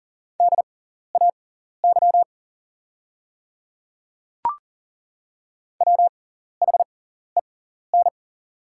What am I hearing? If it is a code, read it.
Morse code, heard as a keyed tone, about 40 wpm, spaced as for 14 wpm, 700 Hz. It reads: DAY WHEN
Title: 6x Character Spacing - Top 100 Words - 40wpm